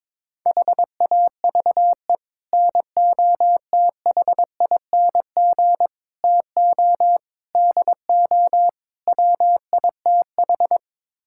Morse code, heard as a keyed tone, about 21 wpm, 700 Hz. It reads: HAVE NOTHING TO DO WITH
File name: practice / 22wpm / Build A Phrase - 5-Word Phrases - 22wpm